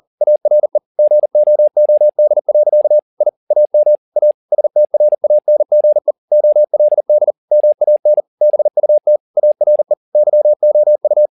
Morse code, heard as a keyed tone, about 40 wpm, 600 Hz. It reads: ARE GOOD. I AM A STRANGE OLD MAN BUT ARE YOU